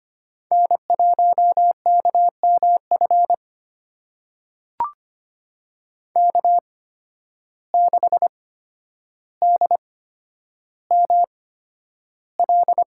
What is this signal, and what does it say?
Morse code, heard as a keyed tone, about 25 wpm, spaced as for 11 wpm, 700 Hz. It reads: N1KMF K6DML
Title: USA Call Signs - 8x Character Spacing - 25wpm